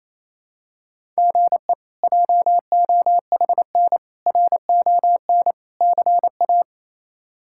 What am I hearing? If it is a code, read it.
Morse code, heard as a keyed tone, about 27 wpm, 700 Hz. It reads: GE JOHN RON CA
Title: K1USN SST QSO Practice - 28wpm